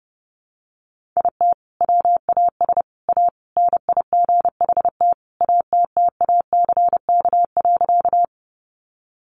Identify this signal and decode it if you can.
Morse code, heard as a keyed tone, about 28 wpm, 700 Hz. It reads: IT WAS A NIGHT ATTACK.
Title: Sentences from Top 1500 Words 30wpm